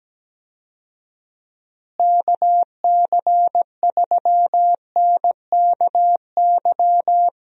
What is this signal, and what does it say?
Morse code, heard as a keyed tone, about 17 wpm, 700 Hz. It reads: KC3NKY
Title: USA Call Signs - 6 Characters - Repeated 3x - 17wpm